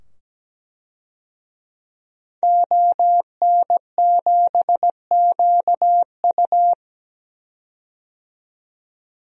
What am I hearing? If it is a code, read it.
Morse code, heard as a keyed tone, about 17 wpm, 700 Hz. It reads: ON7QU
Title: DXCC-100 Call Signs - 5 Characters - 17wpm